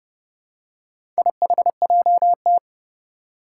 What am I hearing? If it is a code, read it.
Morse code, heard as a keyed tone, about 30 wpm, 700 Hz. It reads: IHJT
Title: Four Challenging Letter-Number Combinations 30wpm